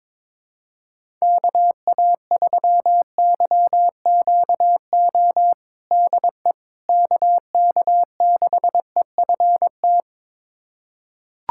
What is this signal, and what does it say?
Morse code, heard as a keyed tone, about 22 wpm, 700 Hz. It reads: KA3YQO DE KK6EFT
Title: USA Callsign DE Callsign - 6 Characters - 22wpm